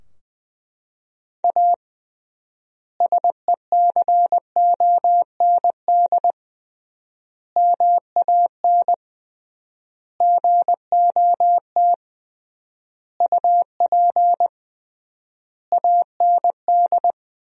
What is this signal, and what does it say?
Morse code, heard as a keyed tone, about 21 wpm, 700 Hz. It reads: A SECOND MAN GOT UP AND
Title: 3x Word-Spacing - Sentences from Top 300 Words - 20wpm